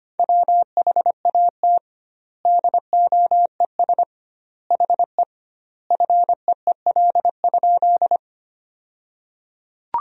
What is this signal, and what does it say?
Morse code, heard as a keyed tone, about 25 wpm, 700 Hz. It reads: WHAT DOES HE FEEL?
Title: 2x Word-Spacing - Sentences from Top 400 Words - 25wpm